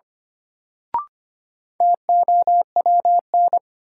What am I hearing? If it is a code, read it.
Morse code, heard as a keyed tone, about 25 wpm, 700 Hz. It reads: TOWN
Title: Sentences from Top 500 Words - New Word Drill - 25wpm